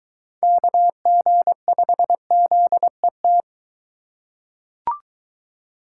Morse code, heard as a keyed tone, about 22 wpm, 700 Hz. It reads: KG5ZET